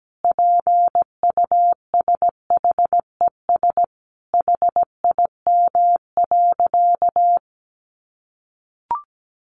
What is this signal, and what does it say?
Morse code, heard as a keyed tone, about 17 wpm, 700 Hz. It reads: PUSHES HIM.
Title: Additional Easy Phrases 17wpm